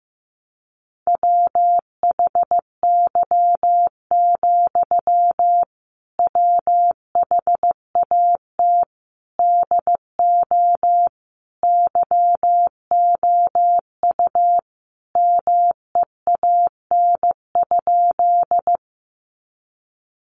Morse code, heard as a keyed tone, about 15 wpm, 700 Hz. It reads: WHY, WHAT DO YOU MEAN?
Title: Sentences from Top 200 Words 15wpm